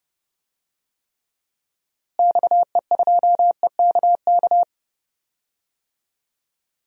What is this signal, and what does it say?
Morse code, heard as a keyed tone, about 31 wpm, 700 Hz. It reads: XE2EKK
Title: DXCC-100 Call Signs - 6 Characters - Repeated 2x - 30wpm